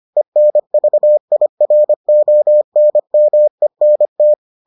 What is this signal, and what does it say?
Morse code, heard as a keyed tone, about 25 wpm, 600 Hz. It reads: ENVIRONMENT